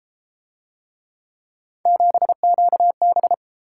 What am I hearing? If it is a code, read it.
Morse code, heard as a keyed tone, about 31 wpm, 700 Hz. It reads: 7QB